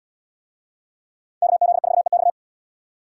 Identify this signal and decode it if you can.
Morse code, heard as a keyed tone, about 77 wpm, 700 Hz. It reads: KZ4EX